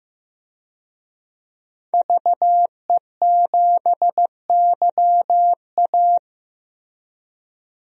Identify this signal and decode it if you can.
Morse code, heard as a keyed tone, about 15 wpm, 700 Hz. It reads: VE7YA
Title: DXCC-100 Call Signs - 5 Characters - Repeated 2x - 15wpm